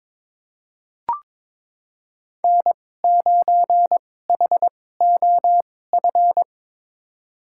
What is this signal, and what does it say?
Morse code, heard as a keyed tone, about 22 wpm, spaced as for 16 wpm, 700 Hz. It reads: N9HOF
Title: USA Call Signs - 2x Character Spacing - 22wpm